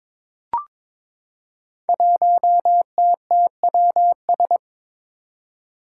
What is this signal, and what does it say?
Morse code, heard as a keyed tone, about 22 wpm, 700 Hz. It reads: 1TTWS
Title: Five Challenging Letter-Number Combinations 22wpm